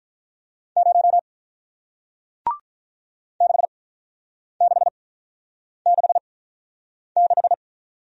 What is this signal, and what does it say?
Morse code, heard as a keyed tone, about 46 wpm, 700 Hz. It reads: 0 6 6 6 6